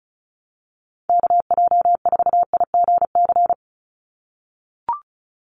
Morse code, heard as a keyed tone, about 36 wpm, 700 Hz. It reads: KJ4IGC
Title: USA Call Signs 35wpm